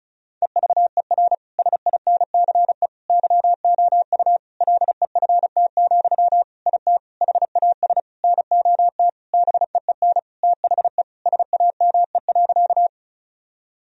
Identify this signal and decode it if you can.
Morse code, heard as a keyed tone, about 34 wpm, 700 Hz. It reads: EVER SINCE YOU LEFT, IT HAS NOT BEEN THE SAME.
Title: Sentences from Top 1000 Words - Review Part 1 - 35wpm